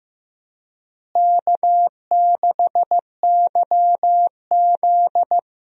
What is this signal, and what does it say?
Morse code, heard as a keyed tone, about 15 wpm, 700 Hz. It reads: K6YZ
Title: USA Call Signs - 4 Characters - Repeated 3x - 15wpm